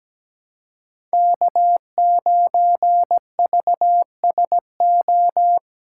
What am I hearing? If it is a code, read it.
Morse code, heard as a keyed tone, about 17 wpm, 700 Hz. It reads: K9VSO
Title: USA Callsign DE Callsign - 5 Characters - 17wpm